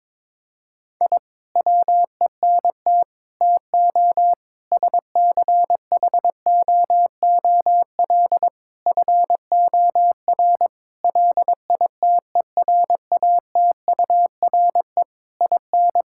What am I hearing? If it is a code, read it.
Morse code, heard as a keyed tone, about 22 wpm, 700 Hz. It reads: I WENT TO SCHOOL FOR LITERATURE IN
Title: Sentences from Top 1000 Words - Review Part 6 - 22wpm